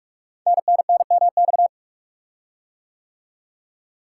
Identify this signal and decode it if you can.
Morse code, heard as a keyed tone, about 44 wpm, 700 Hz. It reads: NNNMX
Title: Code Groups - 5 Letters - 45wpm